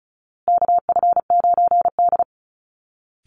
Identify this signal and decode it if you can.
Morse code, heard as a keyed tone, about 34 wpm, 700 Hz. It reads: KF9D
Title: Call Signs Repeated 2x - 35wpm